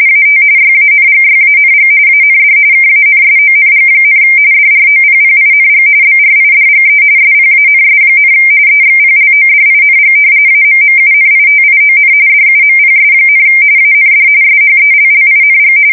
RTTY, about 45.45 baud. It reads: AT YANGYANG AND KANSONG, DROPPING BRIDGE SPANS, EXPLODING AMMUNITION DUMPS, AND SHELLING
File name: fsk170 / battleshipnj